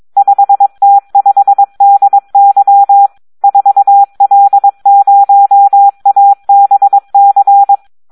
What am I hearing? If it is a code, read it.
Morse code, heard as a keyed tone, about 22 wpm, 800 Hz. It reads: 5T5DY 4L0ABC
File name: Audio practica CW / SD 1 22